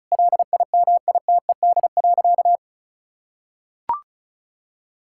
Morse code, heard as a keyed tone, about 36 wpm, 700 Hz. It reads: LIMITED.
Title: Sentences from Top 1000 Words - Review Part 6 - 35wpm